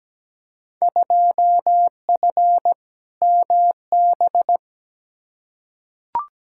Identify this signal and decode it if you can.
Morse code, heard as a keyed tone, about 17 wpm, 700 Hz. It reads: 2F MB
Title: Field Day Exchanges Encore - 17wpm